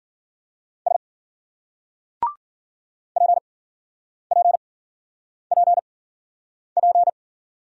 Morse code, heard as a keyed tone, about 52 wpm, spaced as for 9 wpm, 700 Hz. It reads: S PPPP